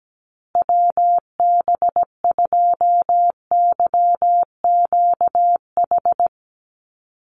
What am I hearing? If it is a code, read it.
Morse code, heard as a keyed tone, about 17 wpm, 700 Hz. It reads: WB2YQH
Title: Call Signs Repeated 2x - 17wpm